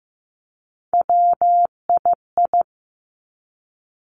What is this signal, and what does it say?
Morse code, heard as a keyed tone, about 15 wpm, 700 Hz. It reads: WII
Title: Three Challenging Letter-Number Combinations 15wpm